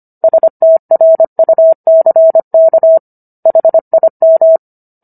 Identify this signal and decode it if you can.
Morse code, heard as a keyed tone, about 25 wpm, 650 Hz. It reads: STRUCK HIM